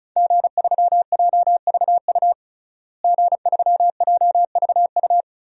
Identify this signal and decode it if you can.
Morse code, heard as a keyed tone, about 36 wpm, 700 Hz. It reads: G3JVU G3JVU
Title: DXCC-100 Call Signs - 5 Characters - Repeated 2x - 35wpm